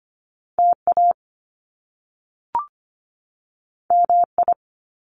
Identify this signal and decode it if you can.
Morse code, heard as a keyed tone, about 25 wpm, 700 Hz. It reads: TA MI
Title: Top Two Letter Combinations in Words 25wpm